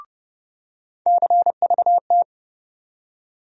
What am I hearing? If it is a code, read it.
Morse code, heard as a keyed tone, about 31 wpm, 700 Hz. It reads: CVT
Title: USA Call Sign Postfixes - 3 Letters 30wpm